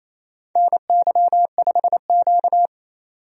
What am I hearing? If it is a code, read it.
Morse code, heard as a keyed tone, about 28 wpm, 700 Hz. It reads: NY5Q
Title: USA Call Signs - 4 Characters - 28wpm